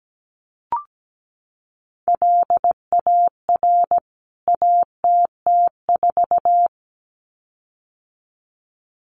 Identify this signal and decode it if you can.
Morse code, heard as a keyed tone, about 17 wpm, 700 Hz. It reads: LAR ATT4
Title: CWT Exchange 17wpm